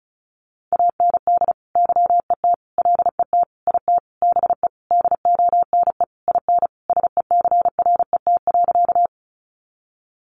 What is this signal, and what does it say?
Morse code, heard as a keyed tone, about 34 wpm, 700 Hz. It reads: AND YET LET IT BE DONE IN SECRET.
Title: Sentences from Top 1200 Words 35wpm